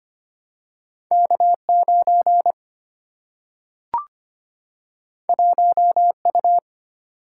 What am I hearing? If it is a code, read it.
Morse code, heard as a keyed tone, about 25 wpm, 700 Hz. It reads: K9 1U
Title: Single-Letter Single-Number 25wpm